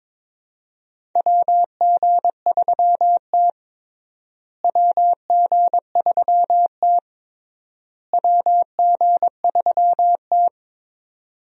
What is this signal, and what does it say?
Morse code, heard as a keyed tone, about 22 wpm, 700 Hz. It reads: WG3T WG3T WG3T
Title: USA Call Signs - 4 Characters - Repeated 3x - 22wpm